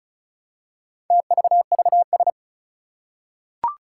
Morse code, heard as a keyed tone, about 36 wpm, 700 Hz. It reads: TVVS
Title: Four Challenging Letter-Number Combinations 35wpm